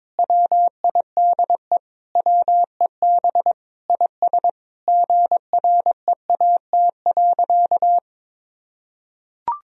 Morse code, heard as a keyed tone, about 21 wpm, 700 Hz. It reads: WIDE WEB IS GREAT.
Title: Sentences from Top 1000 Words - Review Part 4 - 22wpm